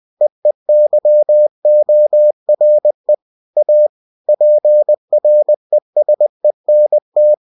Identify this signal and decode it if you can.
Morse code, heard as a keyed tone, about 20 wpm, 600 Hz. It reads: EEYORE A PRESENT
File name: cw-books / winnie-the-pooh / winnie-20wpm-page-0048